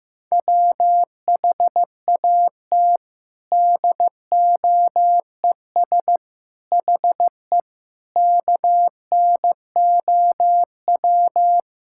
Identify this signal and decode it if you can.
Morse code, heard as a keyed tone, about 15 wpm, 700 Hz. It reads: WHAT DOES HE KNOW
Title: Sentences from Top 1600 Words 15wpm